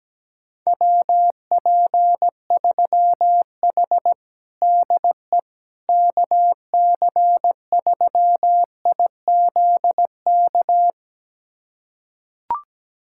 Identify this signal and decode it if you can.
Morse code, heard as a keyed tone, about 17 wpm, 700 Hz. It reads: WP3H DE KC3IZK
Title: USA Callsign DE Callsign 17wpm